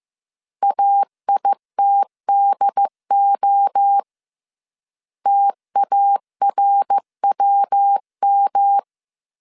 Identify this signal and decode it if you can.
Morse code, heard as a keyed tone, about 15 wpm, 800 Hz. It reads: AITDO TARWM